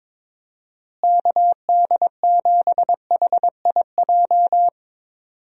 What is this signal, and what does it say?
Morse code, heard as a keyed tone, about 22 wpm, 700 Hz. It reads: KD7HIJ
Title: USA Call Signs - 6 Characters - Repeated 2x - 22wpm